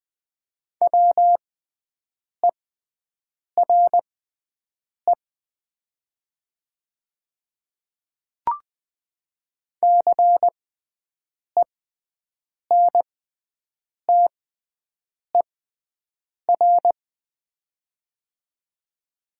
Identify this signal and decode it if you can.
Morse code, heard as a keyed tone, about 20 wpm, spaced as for 7 wpm, 700 Hz. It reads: WERE CENTER
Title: 6x Character Spacing - Top 500 Words - 20wpm